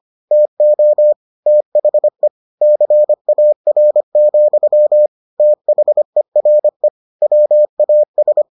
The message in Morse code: TO THE CAR, THERE WAS

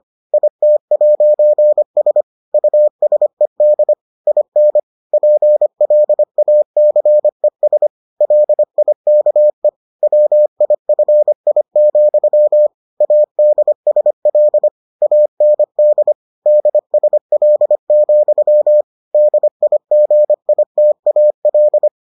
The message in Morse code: IT'S USED IN PLACES LIKE WIFI, ADSL AND DSL, DIGITAL

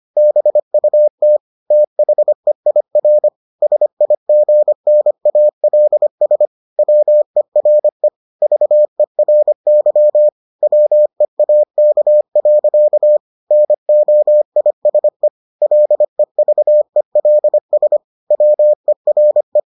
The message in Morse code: BUT THEIR SIGNALS WERE VERY WEAK. NOISE LEVELS WERE

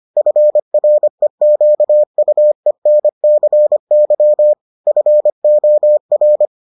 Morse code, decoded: FREQUENCY FOR

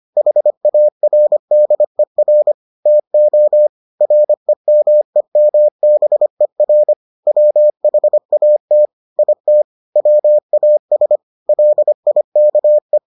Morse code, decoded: HARDER TO REMEMBER WHAT IT WAS LIKE

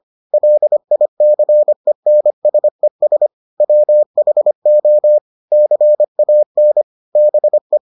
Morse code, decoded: LICENSES WHO CAN BE

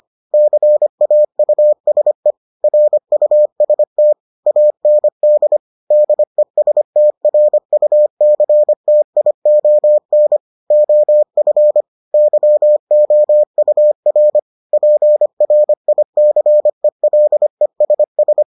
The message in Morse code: CAUSE RUST AND DESTRUCTION OF YOUR PRICELESS

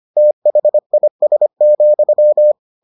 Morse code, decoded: THIS,

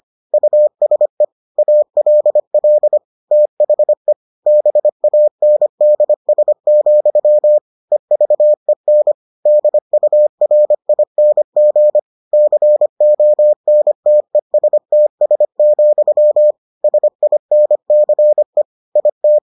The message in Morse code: USE ALL THE BANDS, EVEN DURING CONTESTS, SINCE IT

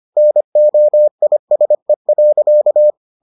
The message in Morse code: NOISE.

